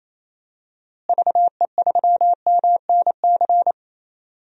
Morse code, heard as a keyed tone, about 28 wpm, 700 Hz. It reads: VE3MNC
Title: DXCC-100 Call Signs - 6 Characters - Repeated 3x - 28wpm